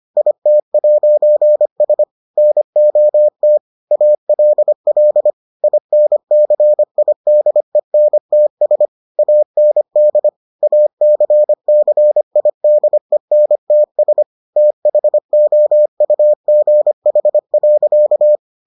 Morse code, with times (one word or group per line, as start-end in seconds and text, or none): 0.2-2.0 IT'S
2.4-3.6 NOT
3.9-5.3 ALL
5.6-8.9 INCIDENTS
9.2-10.3 AND
10.6-14.2 ACCIDENTS
14.6-18.4 THOUGH.